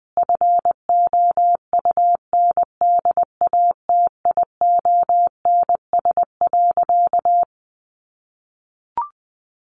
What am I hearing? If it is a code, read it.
Morse code, heard as a keyed tone, about 20 wpm, 700 Hz. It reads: FOUNDATIONS.
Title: A Sound Of Thunder 20wpm